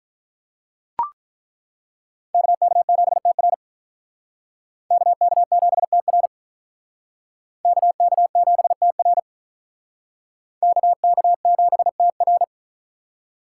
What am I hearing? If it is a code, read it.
Morse code, heard as a keyed tone, about 43 wpm, 700 Hz. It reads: KK7TR KK7TR KK7TR KK7TR
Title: Speed Racing - USA 2x2 Call Signs 35wpm